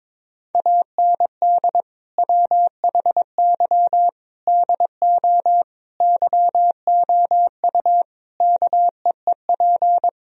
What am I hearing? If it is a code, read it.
Morse code, heard as a keyed tone, about 22 wpm, 700 Hz. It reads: AND WHY DO YOU KEEP
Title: Sentences from Top 1000 Words - Review Part 4 - 22wpm